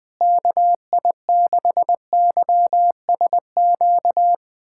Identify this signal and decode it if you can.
Morse code, heard as a keyed tone, about 20 wpm, 700 Hz. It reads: KI6YSQ
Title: USA Callsign DE Callsign - 6 Characters - 20wpm